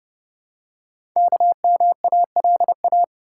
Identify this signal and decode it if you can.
Morse code, heard as a keyed tone, about 30 wpm, 700 Hz. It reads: KMALA
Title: Code Groups - 5 Letters - 30wpm